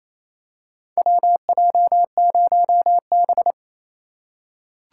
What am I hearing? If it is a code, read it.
Morse code, heard as a keyed tone, about 28 wpm, 700 Hz. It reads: WJ0B
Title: USA Call Signs - 4x Character Spacing - 28wpm